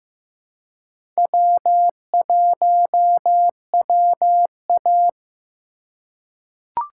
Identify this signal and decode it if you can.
Morse code, heard as a keyed tone, about 15 wpm, 700 Hz. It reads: W1WA
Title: USA Call Signs - 4 Characters - 4x Character Spacing - 15wpm